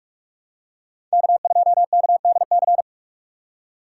Morse code, heard as a keyed tone, about 44 wpm, 700 Hz. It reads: K2KDC